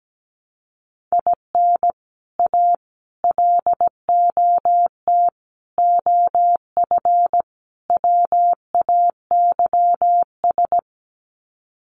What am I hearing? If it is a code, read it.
Morse code, heard as a keyed tone, about 17 wpm, 700 Hz. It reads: IN A LOT OF WAYS
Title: Sets of 5 Words Encore 17wpm